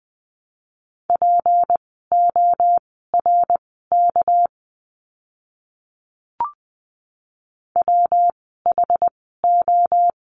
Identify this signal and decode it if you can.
Morse code, heard as a keyed tone, about 20 wpm, spaced as for 15 wpm, 700 Hz. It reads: PORK WHO
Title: 199 Lesson 32f - Letter k - Words 20wpm